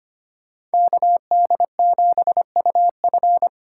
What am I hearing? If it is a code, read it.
Morse code, heard as a keyed tone, about 25 wpm, 700 Hz. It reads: KD7UF
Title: USA Call Signs - 5 Characters - 4x Character Spacing - 25wpm